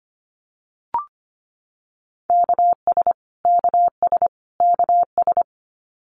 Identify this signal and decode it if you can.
Morse code, heard as a keyed tone, about 25 wpm, 700 Hz. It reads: KS KS KS
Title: US States Repeated 3x 25wpm